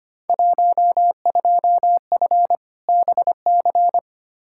Morse code, heard as a keyed tone, about 25 wpm, 700 Hz. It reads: 12F BC
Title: Field Day Exchanges Encore - 25wpm